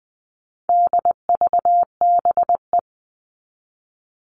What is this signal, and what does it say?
Morse code, heard as a keyed tone, about 20 wpm, 700 Hz. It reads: DVBE
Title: All Cut Numbers - 4-Digit 20wpm